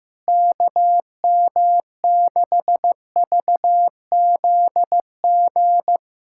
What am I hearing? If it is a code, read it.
Morse code, heard as a keyed tone, about 16 wpm, 700 Hz. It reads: KM6VZG
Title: USA Callsign DE Callsign 15wpm